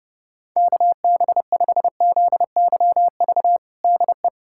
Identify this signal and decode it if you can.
Morse code, heard as a keyed tone, about 30 wpm, 700 Hz. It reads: KB5ZYV DE
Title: USA Callsign DE Callsign - 6 Characters - 30wpm